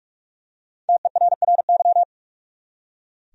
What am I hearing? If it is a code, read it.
Morse code, heard as a keyed tone, about 46 wpm, 700 Hz. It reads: TERRY
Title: USA Common Names - 45wpm